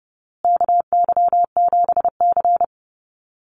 Morse code, heard as a keyed tone, about 30 wpm, 700 Hz. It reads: KY7C
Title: USA Call Signs Repeated 3x 30wpm